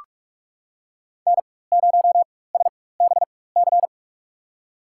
Morse code, heard as a keyed tone, about 41 wpm, spaced as for 20 wpm, 700 Hz. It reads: N0SBC
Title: USA Call Signs - 5 Characters - 4x Character Spacing - 45wpm